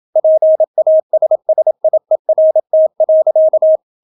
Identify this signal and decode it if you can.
Morse code, heard as a keyed tone, about 27 wpm, 630 Hz. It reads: PASSIERT.